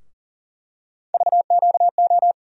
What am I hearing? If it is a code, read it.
Morse code, heard as a keyed tone, about 40 wpm, 700 Hz. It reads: VQO
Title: USA Call Sign Postfixes - 3 Letters 40wpm